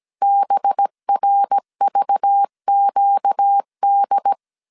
Morse code, heard as a keyed tone, about 17 wpm, 800 Hz. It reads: BRVQD
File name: CW / DET / 1 6 1500C S A 16wpm 2635